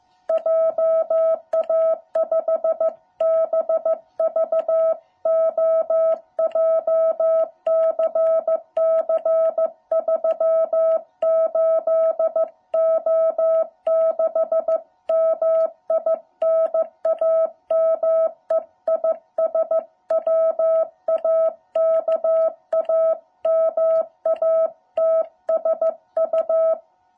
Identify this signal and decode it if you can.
Morse code, heard as a keyed tone, about 14 wpm, 660 Hz. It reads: JA5BVO JCC 38O6 MI NAME IS WAKAMATSU